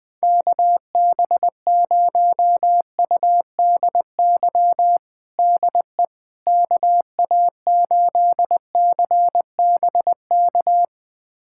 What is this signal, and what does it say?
Morse code, heard as a keyed tone, about 21 wpm, 700 Hz. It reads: KB0UDY DE KA8CBK
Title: USA Callsign DE Callsign - 6 Characters - 20wpm